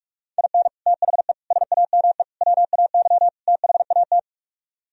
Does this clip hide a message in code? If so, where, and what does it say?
Yes: Morse code from 0.4 to 4.2 s, heard as a keyed tone, about 47 wpm, 700 Hz. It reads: IN THE SAME WAY THAT